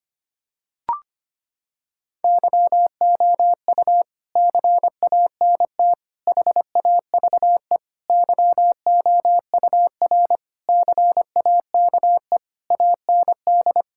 Morse code, heard as a keyed tone, about 25 wpm, 700 Hz. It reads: YOU CANT HAVE YOUR CAKE AND
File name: Sets of 9 Words - 25wpm